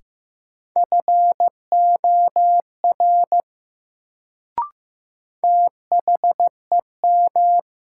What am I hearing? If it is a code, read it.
Morse code, heard as a keyed tone, about 15 wpm, 700 Hz. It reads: FOR THEM
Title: Sentences from Top 100 Words - New Word Drill - 15wpm